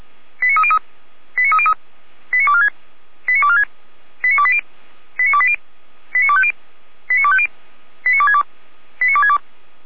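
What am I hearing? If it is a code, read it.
Five-tone selective call (ZVEI), heard as quick runs of five tones, tones of about 70 ms. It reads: 89303 89303 89246 89246 89280 89280 89250 89250 89262 89262
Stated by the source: ZVEI five-tone selective call 89235 received with an SDR, recorded by feuerwehr-orlamuende (Orlamünde fire brigade).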